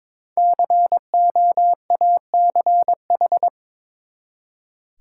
Morse code, heard as a keyed tone, about 22 wpm, 700 Hz. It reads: COACH
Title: Five Letter Words - 22wpm